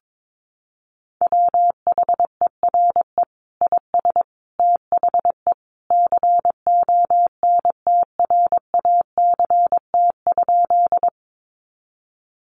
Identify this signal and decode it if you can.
Morse code, heard as a keyed tone, about 22 wpm, 700 Hz. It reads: WHERE IS THE CONTRACT?
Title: Sentences from Top 1400 Words 22wpm